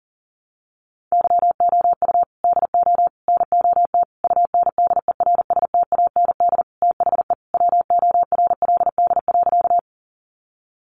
Morse code, heard as a keyed tone, about 37 wpm, 700 Hz. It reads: YOU DO NOT UNDERSTAND THE WORLD.
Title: Sentences from Top 600 Words 40wpm